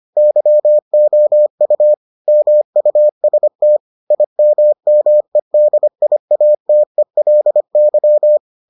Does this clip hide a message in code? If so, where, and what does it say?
Yes: Morse code from 0.2 to 8.4 s, heard as a keyed tone, about 25 wpm, 600 Hz. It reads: YOU MUST IMMEDIATELY